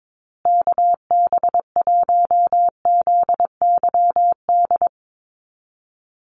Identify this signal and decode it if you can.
Morse code, heard as a keyed tone, about 22 wpm, 700 Hz. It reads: KB1ZYD
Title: USA Call Signs Repeated 3x 22wpm